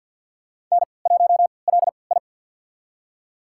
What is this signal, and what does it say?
Morse code, heard as a keyed tone, about 51 wpm, spaced as for 28 wpm, 700 Hz. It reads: N1LI
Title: USA Call Signs - 4 Characters - 3x Character Spacing - 50wpm